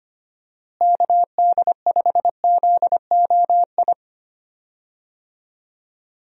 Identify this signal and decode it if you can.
Morse code, heard as a keyed tone, about 25 wpm, 700 Hz. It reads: KD5ZOI
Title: USA Call Signs - 6 Characters - Repeated 3x - 25wpm